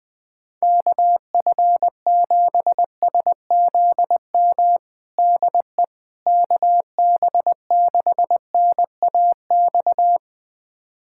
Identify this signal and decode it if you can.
Morse code, heard as a keyed tone, about 21 wpm, 700 Hz. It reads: KF7SZM DE KB6NAX
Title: USA Callsign DE Callsign - 6 Characters - 20wpm